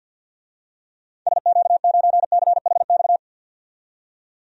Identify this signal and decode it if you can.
Morse code, heard as a keyed tone, about 46 wpm, 700 Hz. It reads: SQ9CHX
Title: DXCC-100 Call Signs - 6 Characters - 50wpm